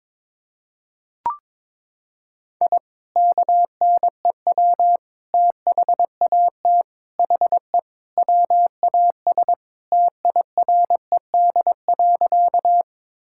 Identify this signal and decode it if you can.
Morse code, heard as a keyed tone, about 22 wpm, 700 Hz. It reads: I KNEW THAT HE WAS TIRED.
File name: practice / 22wpm / Sentences with 3-Word Phrases - 22wpm